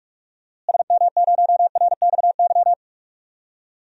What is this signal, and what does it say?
Morse code, heard as a keyed tone, about 44 wpm, 700 Hz. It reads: SM0RXY
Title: DXCC-100 Call Signs - 6 Characters - Repeated 3x - 45wpm